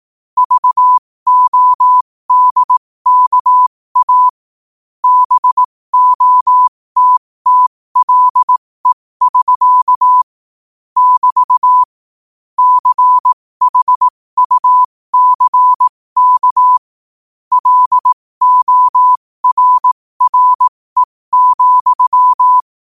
Morse code, heard as a keyed tone, about 18 wpm, spaced as for 15 wpm, 1000 Hz. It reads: VODKA BOTTLE<SK> <BT> CHUCK LORRE,
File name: qotdfiles / 15wpm